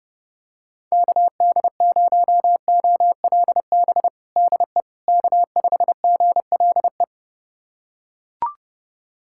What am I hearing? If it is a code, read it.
Morse code, heard as a keyed tone, about 30 wpm, 700 Hz. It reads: KD0OLB DE K5GLE
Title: USA Callsign DE Callsign 30wpm